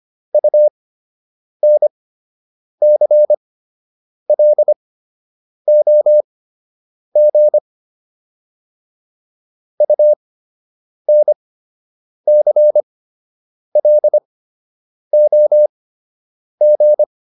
Morse code, heard as a keyed tone, about 25 wpm, spaced as for 8 wpm, 600 Hz. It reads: UNCLOG UNCLOG